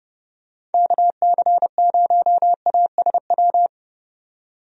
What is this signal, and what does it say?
Morse code, heard as a keyed tone, about 30 wpm, 700 Hz. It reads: KC0ASW
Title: USA Call Signs - 6 Characters - 30wpm